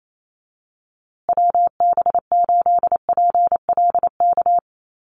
Morse code, heard as a keyed tone, about 28 wpm, 700 Hz. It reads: WB8PLK
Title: USA Call Signs Repeated 2x 28wpm